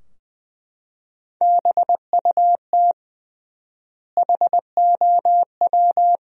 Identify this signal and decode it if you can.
Morse code, heard as a keyed tone, about 20 wpm, 700 Hz. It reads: BUT HOW
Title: 3x Word-Spacing - Sentences from Top 500 Words - 20wpm